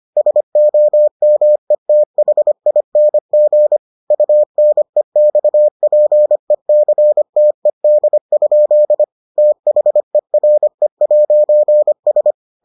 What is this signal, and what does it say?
Morse code, heard as a keyed tone, about 25 wpm, 600 Hz. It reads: SOMETHING UNEXPECTED? THERE'S